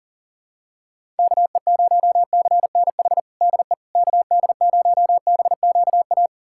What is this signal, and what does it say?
Morse code, heard as a keyed tone, about 40 wpm, 700 Hz. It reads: KE0CNH DE KD0BQA
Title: USA Callsign DE Callsign - 6 Characters - 40wpm